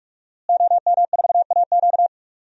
Morse code, heard as a keyed tone, about 44 wpm, 700 Hz. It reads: OM4AQ
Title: DXCC-100 Call Signs - 5 Characters - Repeated 3x - 45wpm